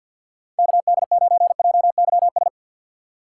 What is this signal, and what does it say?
Morse code, heard as a keyed tone, about 47 wpm, 700 Hz. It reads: KD9JYS
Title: USA Call Signs - 6 Characters - 4x Character Spacing - 50wpm